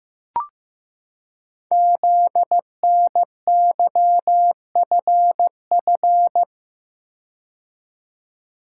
Morse code, heard as a keyed tone, about 15 wpm, 700 Hz. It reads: ZNYFF
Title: Code Groups - 5 Letters - 15wpm